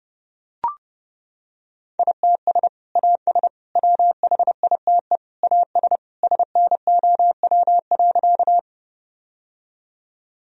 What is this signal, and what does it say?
Morse code, heard as a keyed tone, about 32 wpm, 700 Hz. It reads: ITS AS WHITE AS SNOW.
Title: Sentences from Top 1000 Words - Review Part 3 - 30wpm